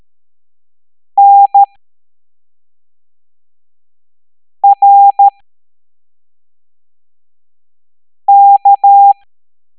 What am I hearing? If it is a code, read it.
Morse code, heard as a keyed tone, about 13 wpm, spaced as for 3 wpm, 800 Hz. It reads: NRK
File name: hamdata / L01-EX02-ETANTK